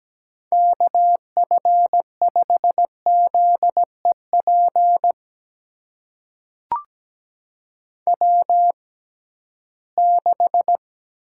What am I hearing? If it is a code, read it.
Morse code, heard as a keyed tone, about 17 wpm, spaced as for 13 wpm, 700 Hz. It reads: KF5ZEP W6
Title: USA Call Signs - 6x Character Spacing - 17wpm